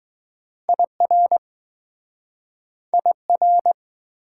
Morse code, heard as a keyed tone, about 20 wpm, 700 Hz. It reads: IR IR